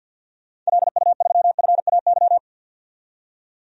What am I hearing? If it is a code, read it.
Morse code, heard as a keyed tone, about 52 wpm, 700 Hz. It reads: LU3FAY
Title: DXCC-100 Call Signs - 6 Characters - 50wpm